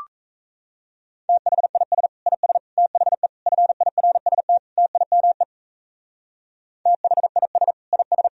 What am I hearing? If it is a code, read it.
Morse code, heard as a keyed tone, about 35 wpm, 700 Hz. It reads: THIS IS THE FIRST TIME THIS IS